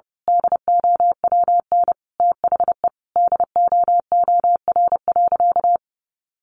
Morse code, heard as a keyed tone, about 30 wpm, 700 Hz. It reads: DOWN THE DOOR.
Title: Sentences from Top 900 Words 30wpm